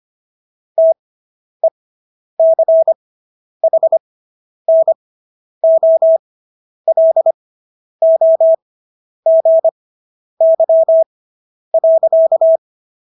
Morse code, heard as a keyed tone, about 25 wpm, spaced as for 10 wpm, 650 Hz. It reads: TECHNOLOGY.